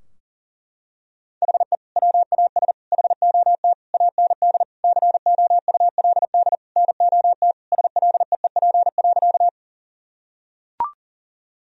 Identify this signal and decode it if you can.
Morse code, heard as a keyed tone, about 40 wpm, 700 Hz. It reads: HE WAS HOT AND COULD NOT SLEEP.
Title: Sentences from Top 1000 Words - Review Part 3 - 40wpm